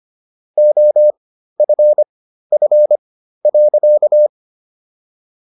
Morse code, heard as a keyed tone, about 25 wpm, spaced as for 13 wpm, 600 Hz. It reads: OFF.